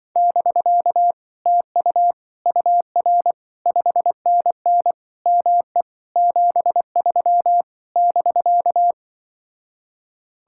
Morse code, heard as a keyed tone, about 24 wpm, 700 Hz. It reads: <BK> TU UR 5NN ME 73 <BK>
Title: Speed-Racing Common QSO Phrases - 20wpm